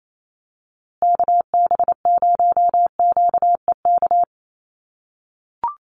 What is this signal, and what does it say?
Morse code, heard as a keyed tone, about 28 wpm, 700 Hz. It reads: KB0QEK
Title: USA Call Signs Repeated 2x 28wpm